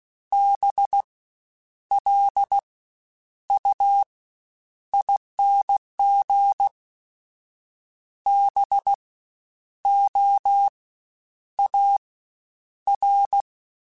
Morse code, heard as a keyed tone, about 16 wpm, spaced as for 9 wpm, 800 Hz. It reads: BLUING BOAR
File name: Special INGs-mp3s / Special INGs-mp3s / Fast ing @15